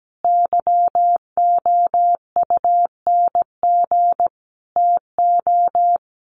Morse code, heard as a keyed tone, about 17 wpm, 700 Hz. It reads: YOUNG TO
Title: Sentences from Top 600 Words 17wpm